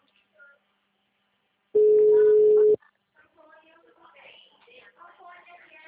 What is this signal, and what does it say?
Ringback tone (Europe, 425 Hz), with speech.
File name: out- -2009- - - .369